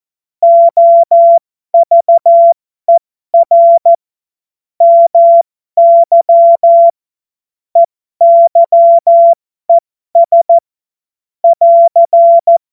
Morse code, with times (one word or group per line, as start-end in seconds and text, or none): 0.4-3.9 OVER
4.8-6.9 MY
7.8-10.6 EYES
11.4-12.6 <AR>